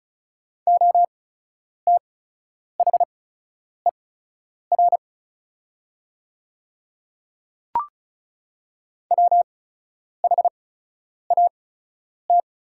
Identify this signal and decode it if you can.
Morse code, heard as a keyed tone, about 36 wpm, spaced as for 10 wpm, 700 Hz. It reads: OTHER WHAT
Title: 8x Character Spacing - Top 100 Words - 35wpm